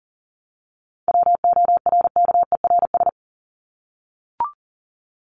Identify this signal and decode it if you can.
Morse code, heard as a keyed tone, about 40 wpm, 700 Hz. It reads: WORKERS
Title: ES:S Suffix - Top 100 Words with Base Words 40wpm